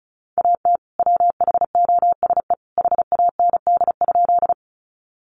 Morse code, heard as a keyed tone, about 34 wpm, 700 Hz. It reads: AT WHOSE HAND?
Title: Sentences from Top 1200 Words 35wpm